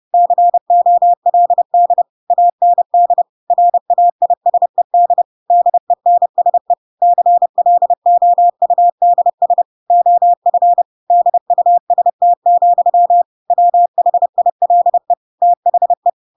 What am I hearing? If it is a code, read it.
Morse code, heard as a keyed tone, about 30 wpm, 700 Hz. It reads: COLD AND RAISED DENSE CLOUDS OF DUST, WHILE THE
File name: Pinocchio / Pinocchio ch6 30 30 ik1hge